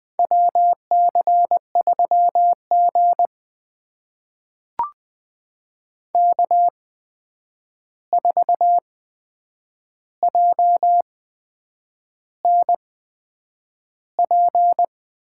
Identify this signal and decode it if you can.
Morse code, heard as a keyed tone, about 20 wpm, spaced as for 8 wpm, 700 Hz. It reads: WC3G K4JNP
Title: USA Call Signs - 8x Character Spacing - 20wpm